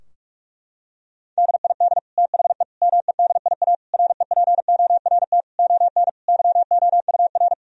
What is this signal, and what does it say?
Morse code, heard as a keyed tone, about 46 wpm, 700 Hz. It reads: DID THE MEDIA REPORT ON YOUR
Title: Sentences from Top 1000 Words - Review Part 4 - 45wpm